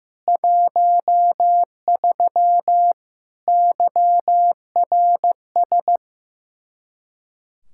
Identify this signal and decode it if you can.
Morse code, heard as a keyed tone, about 15 wpm, 700 Hz. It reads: 13 YRS
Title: Practice QSO - Exchange 3 - More Info - 15wpm